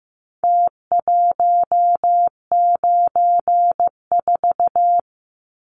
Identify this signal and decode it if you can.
Morse code, heard as a keyed tone, about 15 wpm, 700 Hz. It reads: T194
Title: Numbers - Contest Style - 4-Digit 15wpm